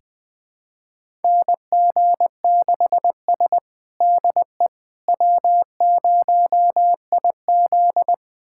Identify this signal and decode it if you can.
Morse code, heard as a keyed tone, about 20 wpm, 700 Hz. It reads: NG6S DE W0IZ
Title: USA Callsign DE Callsign - 4 Characters - 20wpm